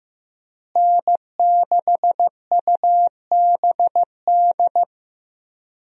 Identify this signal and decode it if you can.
Morse code, heard as a keyed tone, about 15 wpm, 700 Hz. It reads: N6UBD
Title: USA Call Signs - 5 Characters - Repeated 3x - 15wpm